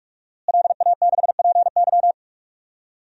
Morse code, heard as a keyed tone, about 44 wpm, 700 Hz. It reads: LA6PY
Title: DXCC-100 Call Signs - 5 Characters - Repeated 2x - 45wpm